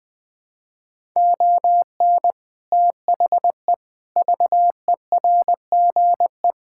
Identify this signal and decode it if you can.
Morse code, heard as a keyed tone, about 21 wpm, 700 Hz. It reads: ON THE VERGE
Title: Build A Phrase - 4-Word Phrases - 20wpm